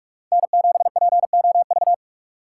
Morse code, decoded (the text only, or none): N7POV